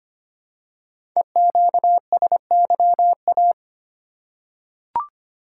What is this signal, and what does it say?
Morse code, heard as a keyed tone, about 25 wpm, 700 Hz. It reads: EQSYA
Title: Code Groups - 5 Letters - 25wpm